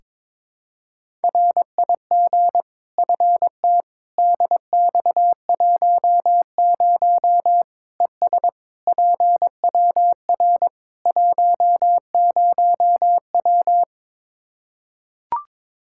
Morse code, heard as a keyed tone, about 22 wpm, 700 Hz. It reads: RIG FT DX10 ES PWR 10W
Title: Practice QSO - Exchange 2 - Info - 22wpm